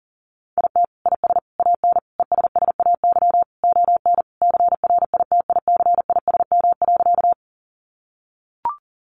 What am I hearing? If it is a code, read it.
Morse code, heard as a keyed tone, about 40 wpm, 700 Hz. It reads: IT IS AN ESSAY ON CRITICISM.
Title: Sentences from Top 1500 Words 40wpm